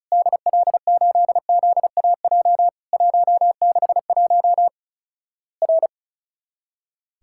Morse code, heard as a keyed tone, about 36 wpm, 700 Hz. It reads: DL8ZAJ 161 R
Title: Worked All Europe DX Contest - QTC Exchange - 35wpm